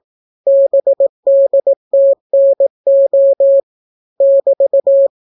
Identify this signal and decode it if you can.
Morse code, heard as a keyed tone, about 18 wpm, 550 Hz. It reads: BDTNO <BT>